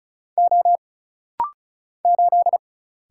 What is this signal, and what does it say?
Morse code, heard as a keyed tone, about 35 wpm, 700 Hz. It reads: O 8
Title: Single Letter-Number Alt - Rapid-Fire 35wpm